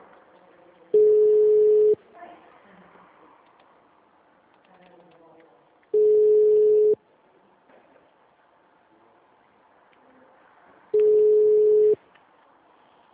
Ringback tone (Europe, 425 Hz).